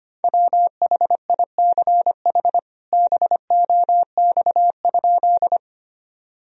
Morse code, heard as a keyed tone, about 25 wpm, 700 Hz. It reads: WHICH BOX?
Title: Sentences from Top 1000 Words - Review Part 2 - 25wpm